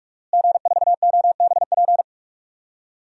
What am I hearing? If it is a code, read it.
Morse code, heard as a keyed tone, about 46 wpm, 700 Hz. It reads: G4OBP
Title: DXCC-100 Call Signs - 5 Characters - Repeated 2x - 45wpm